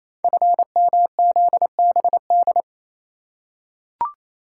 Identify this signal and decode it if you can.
Morse code, heard as a keyed tone, about 28 wpm, 700 Hz. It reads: FMZBD
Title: Code Groups - 5 Letters - 28wpm